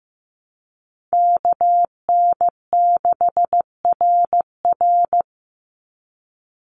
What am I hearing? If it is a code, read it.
Morse code, heard as a keyed tone, about 15 wpm, 700 Hz. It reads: KN6RR
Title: USA Call Signs 15wpm